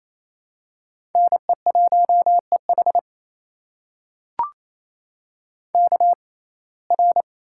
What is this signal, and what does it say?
Morse code, heard as a keyed tone, about 28 wpm, spaced as for 20 wpm, 700 Hz. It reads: NE1EH KR
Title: USA Call Signs - 5 Characters - 6x Character Spacing - 28wpm